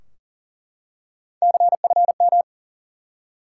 Morse code, heard as a keyed tone, about 40 wpm, 700 Hz. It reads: CFM
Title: Top 90 QSO Elements 40wpm